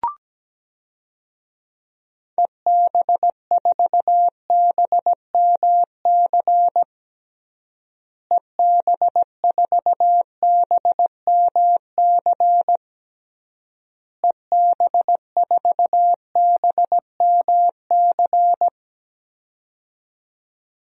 Morse code, heard as a keyed tone, about 17 wpm, 700 Hz. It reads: EB4BMC EB4BMC EB4BMC
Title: DXCC-100 Call Signs - 6 Characters - Repeated 3x - 17wpm